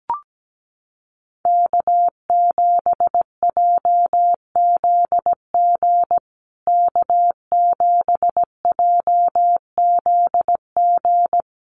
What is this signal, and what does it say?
Morse code, heard as a keyed tone, about 17 wpm, 700 Hz. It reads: K7JZG K7JZG
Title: USA Call Signs Repeated 3x 17wpm